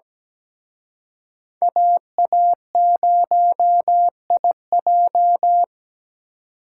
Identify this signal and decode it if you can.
Morse code, heard as a keyed tone, about 20 wpm, 700 Hz. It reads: AA0IJ
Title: Speed Racing - USA 2x2 Call Signs 17wpm